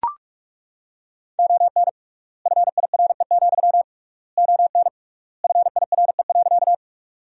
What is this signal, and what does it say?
Morse code, heard as a keyed tone, about 42 wpm, 700 Hz. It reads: ON FIRE, ON FIRE.
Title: 3x Word-Spacing - Sentences from Top 400 Words - 45wpm